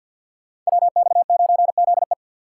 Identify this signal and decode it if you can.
Morse code, heard as a keyed tone, about 51 wpm, 700 Hz. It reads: WX9ZE
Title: Code Groups - 5 Letter:Numbers - 50wpm